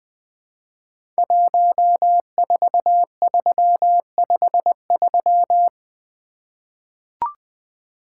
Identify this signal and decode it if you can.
Morse code, heard as a keyed tone, about 20 wpm, 700 Hz. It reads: 14353
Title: Code Groups - 5 Letter:Numbers - 20wpm